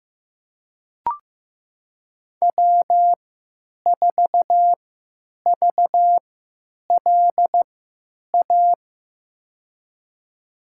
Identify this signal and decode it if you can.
Morse code, heard as a keyed tone, about 15 wpm, spaced as for 9 wpm, 700 Hz. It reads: W4VLA
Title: USA Call Signs - 5 Characters - 3x Character Spacing - 15wpm